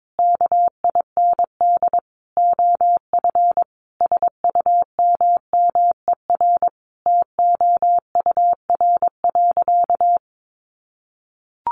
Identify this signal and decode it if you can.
Morse code, heard as a keyed tone, about 22 wpm, 700 Hz. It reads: KIND OF SUMMER TOUR.
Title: Sentences from Top 1400 Words 22wpm